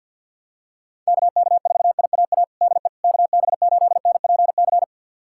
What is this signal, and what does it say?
Morse code, heard as a keyed tone, about 50 wpm, 690 Hz. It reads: KK4IAA DE KB8NPC
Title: USA Callsign DE Callsign - 6 Characters - 50wpm